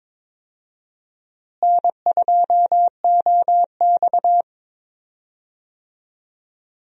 Morse code, heard as a keyed tone, about 22 wpm, 700 Hz. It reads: N2OX
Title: USA Call Signs - 4 Characters - Repeated 3x - 22wpm